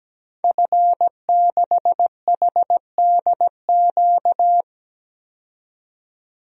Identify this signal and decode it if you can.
Morse code, heard as a keyed tone, about 17 wpm, 700 Hz. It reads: F6HDQ
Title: DXCC-100 Call Signs - 5 Characters - 17wpm